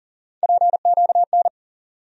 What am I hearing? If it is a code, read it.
Morse code, heard as a keyed tone, about 40 wpm, 700 Hz. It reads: PQN
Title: USA Call Sign Postfixes - 3 Letters 40wpm